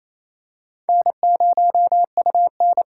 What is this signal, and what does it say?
Morse code, heard as a keyed tone, about 27 wpm, 690 Hz. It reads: N0UN
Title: USA Call Signs - 4 Characters - 2x Character Spacing - 28wpm